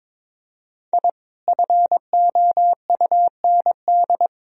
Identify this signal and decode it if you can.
Morse code, heard as a keyed tone, about 22 wpm, 700 Hz. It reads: I FOUND
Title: Sentences from Top 1000 Words - Review Part 1 - 22wpm